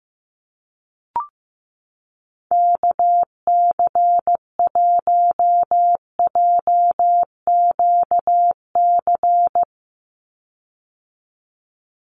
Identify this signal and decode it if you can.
Morse code, heard as a keyed tone, about 15 wpm, 700 Hz. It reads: KC1JQC
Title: USA Call Signs 15wpm